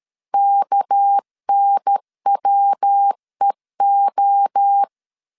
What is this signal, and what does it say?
Morse code, heard as a keyed tone, about 13 wpm, 800 Hz. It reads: KNWEO
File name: LEZIONI / 6 PKX0 / 6 PK 12wpm 2006